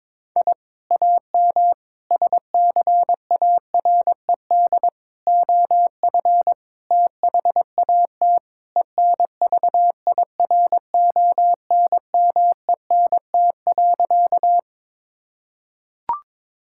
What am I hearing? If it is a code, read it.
Morse code, heard as a keyed tone, about 22 wpm, 700 Hz. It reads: I AM SCARED OF THAT ENVIRONMENT.
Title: Sentences from Top 1000 Words - Review Part 5 - 22wpm